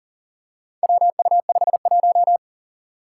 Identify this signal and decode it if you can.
Morse code, heard as a keyed tone, about 43 wpm, 700 Hz. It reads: WU51
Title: Four Challenging Letter-Number Combinations 40wpm